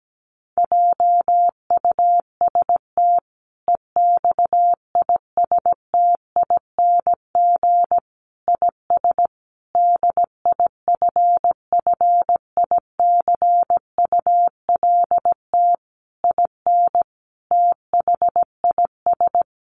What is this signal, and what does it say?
Morse code, heard as a keyed tone, about 17 wpm, 700 Hz. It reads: JUST EXISTING IS DIFFICULT IN THIS
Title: Sentences from Top 1500 Words 17wpm